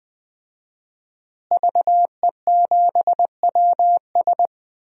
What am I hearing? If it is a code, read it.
Morse code, heard as a keyed tone, about 20 wpm, 700 Hz. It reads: VE7WS
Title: DXCC-100 Call Signs - 5 Characters - Repeated 2x - 20wpm